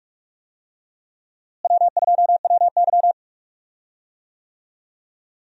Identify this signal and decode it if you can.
Morse code, heard as a keyed tone, about 44 wpm, 700 Hz. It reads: W2WY
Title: USA Call Signs - 4 Characters - Repeated 3x - 45wpm